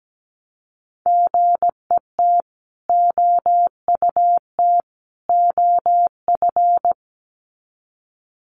Morse code, heard as a keyed tone, about 17 wpm, 700 Hz. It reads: GET OUT OF
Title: Sets of 3 Words 17wpm